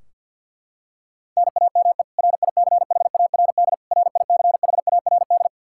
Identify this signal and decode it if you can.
Morse code, heard as a keyed tone, about 51 wpm, 700 Hz. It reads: NAME RICHARD RICHARD
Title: Practice QSO - Exchange 1 - Beginning - 50wpm